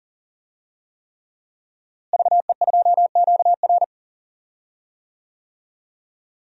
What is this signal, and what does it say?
Morse code, heard as a keyed tone, about 40 wpm, 700 Hz. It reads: VE2QR
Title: DXCC-100 Call Signs - 5 Characters - Repeated 3x - 40wpm